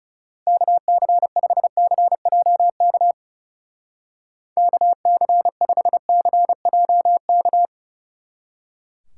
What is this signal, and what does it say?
Morse code, heard as a keyed tone, about 37 wpm, 700 Hz. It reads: KC5CJK KC5CJK
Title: Speed Racing - USA 2x3 Call Signs 30wpm